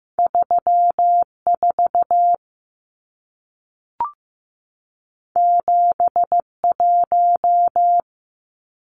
Morse code, heard as a keyed tone, about 15 wpm, 700 Hz. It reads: 34 71
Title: Numbers - 2-Digit 15wpm